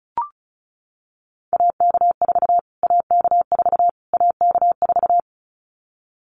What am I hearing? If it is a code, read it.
Morse code, heard as a keyed tone, about 34 wpm, 700 Hz. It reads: AK4 AK4 AK4
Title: Callsign Prefixes for Top 25 DXCC - Repeated 3x - 35wpm